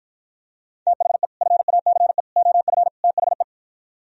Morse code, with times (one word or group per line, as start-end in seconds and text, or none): 0.9-1.3 THE
1.4-2.2 FACE
2.4-2.9 OF
3.0-3.4 THE